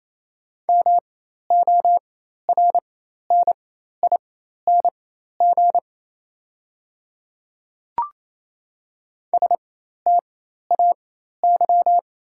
Morse code, heard as a keyed tone, about 27 wpm, spaced as for 13 wpm, 700 Hz. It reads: MORNING STAY
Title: 4x Character Spacing - Top 500 Words - 28wpm